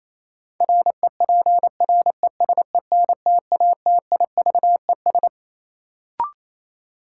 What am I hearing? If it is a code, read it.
Morse code, heard as a keyed tone, about 28 wpm, 700 Hz. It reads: REPRESENTATIVES
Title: Sets of 5 Words Encore - New Word Drill - 28wpm